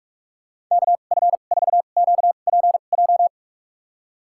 Morse code, heard as a keyed tone, about 46 wpm, spaced as for 33 wpm, 700 Hz. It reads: KF4QPJ
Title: USA Call Signs - 6 Characters - 2x Character Spacing - 45wpm